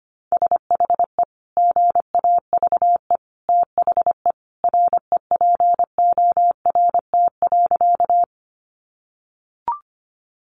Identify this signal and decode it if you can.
Morse code, heard as a keyed tone, about 25 wpm, 700 Hz. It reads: SHE GAVE THE REPORT.
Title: Sentences from Top 900 Words 25wpm